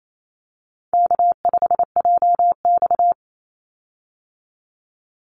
Morse code, heard as a keyed tone, about 28 wpm, 700 Hz. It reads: K5JX
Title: Easy Call Signs - 28wpm